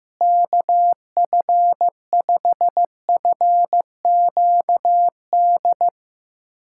Morse code, heard as a keyed tone, about 15 wpm, 700 Hz. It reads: KF5FQD
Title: USA Callsign DE Callsign - 6 Characters - 15wpm